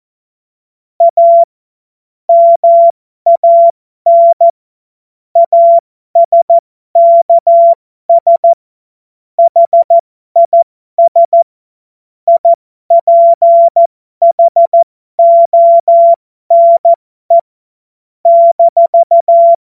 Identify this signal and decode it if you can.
Morse code, heard as a keyed tone, about 14 wpm, spaced as for 12 wpm, 680 Hz. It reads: A MAN ASKS HIS IPHONE -
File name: Morsekurs / Witze